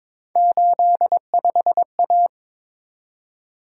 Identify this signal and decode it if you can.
Morse code, heard as a keyed tone, about 22 wpm, 700 Hz. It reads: 85A
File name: practice / 22wpm / Three Letter-Number Combinations - 22wpm